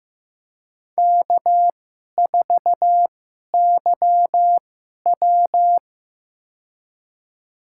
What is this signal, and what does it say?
Morse code, heard as a keyed tone, about 15 wpm, spaced as for 11 wpm, 700 Hz. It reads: K4YW
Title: USA Call Signs - 4 Characters - 2x Character Spacing - 15wpm